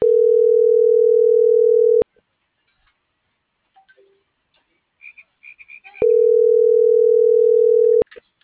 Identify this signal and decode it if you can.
Ringback tone (North America), 440 + 480 Hz.